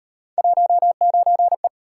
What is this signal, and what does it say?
Morse code, heard as a keyed tone, about 38 wpm, 700 Hz. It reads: 19E